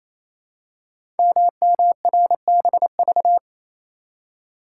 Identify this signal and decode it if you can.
Morse code, heard as a keyed tone, about 29 wpm, 700 Hz. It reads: MMRBV
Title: Code Groups - 5 Letters - 28wpm